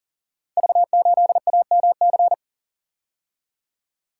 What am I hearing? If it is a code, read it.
Morse code, heard as a keyed tone, about 38 wpm, 690 Hz. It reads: V8AMC
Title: Code Groups - 5 Letter:Numbers - 40wpm